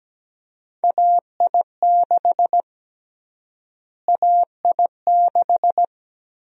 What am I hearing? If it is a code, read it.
Morse code, heard as a keyed tone, about 17 wpm, 700 Hz. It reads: AI6 AI6
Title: USA Call Sign Prefixes - Repeated 2x - 17wpm